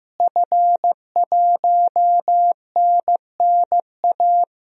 Morse code, heard as a keyed tone, about 15 wpm, 700 Hz. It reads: F1NNA
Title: DXCC-100 Call Signs - 5 Characters - Repeated 3x - 15wpm